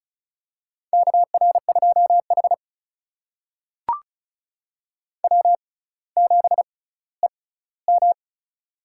Morse code, heard as a keyed tone, about 36 wpm, spaced as for 18 wpm, 700 Hz. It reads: KR2H W7EM
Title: USA Call Signs - 4 Characters - 6x Character Spacing - 35wpm